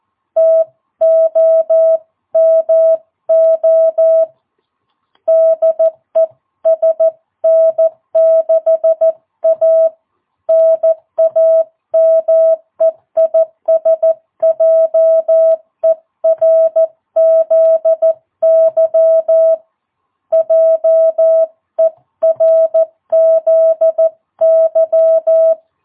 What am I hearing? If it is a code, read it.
Morse code, heard as a keyed tone, about 14 wpm, 660 Hz. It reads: TOMO DE SN6A NAME IS JERZY JERZY
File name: jh8cbh / 2022cw